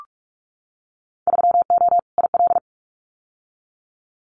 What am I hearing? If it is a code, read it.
Morse code, heard as a keyed tone, about 46 wpm, 700 Hz. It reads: 3O IL
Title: Winter Field Day Exchanges 45wpm